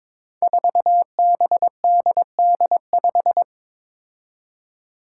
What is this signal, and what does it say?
Morse code, heard as a keyed tone, about 22 wpm, 700 Hz. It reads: 4BDD5
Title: Five Challenging Letter-Number Combinations 22wpm